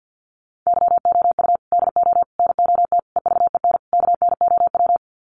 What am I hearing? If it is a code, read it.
Morse code, heard as a keyed tone, about 50 wpm, 700 Hz. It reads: YOU DO NOT EVEN KNOW
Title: Sets of 5 Words Encore2 50wpm